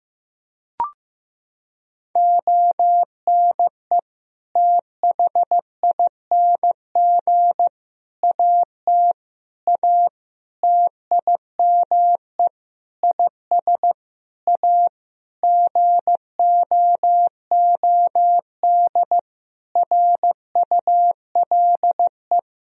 Morse code, heard as a keyed tone, about 16 wpm, 700 Hz. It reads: ONE THING AT A TIME IS A GOOD RULE
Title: Sentences from Top 1000 Words - Review Part 2 - 15wpm